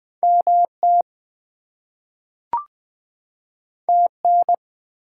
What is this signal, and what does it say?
Morse code, heard as a keyed tone, about 20 wpm, 700 Hz. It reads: MT TN
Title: US States and Canadian Provinces 20wpm